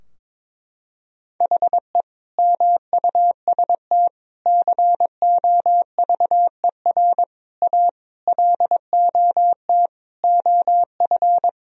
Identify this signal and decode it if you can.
Morse code, heard as a keyed tone, about 22 wpm, 700 Hz. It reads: HE MUST COVER A LOT OF
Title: Sentences from Top 1000 Words - Review Part 3 - 22wpm